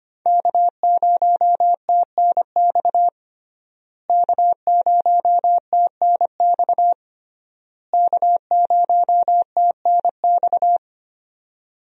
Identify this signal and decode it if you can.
Morse code, heard as a keyed tone, about 25 wpm, 700 Hz. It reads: K0TNX K0TNX K0TNX
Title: USA Call Signs - 5 Characters - Repeated 3x - 25wpm